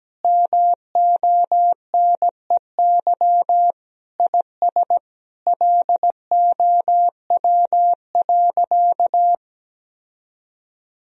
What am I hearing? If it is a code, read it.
Morse code, heard as a keyed tone, about 17 wpm, 700 Hz. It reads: MONEY IS LOW.
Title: Sentences from Top 1000 Words - Review Part 1 - 17wpm